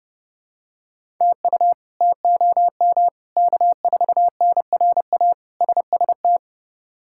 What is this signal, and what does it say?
Morse code, heard as a keyed tone, about 30 wpm, 700 Hz. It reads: TU TOM K4NRA SST
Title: K1USN SST QSO Practice - 30wpm